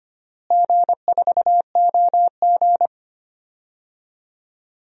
Morse code, heard as a keyed tone, about 25 wpm, 700 Hz. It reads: G4OG
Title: DXCC-100 Call Signs - 4 Characters - Repeated 3x - 25wpm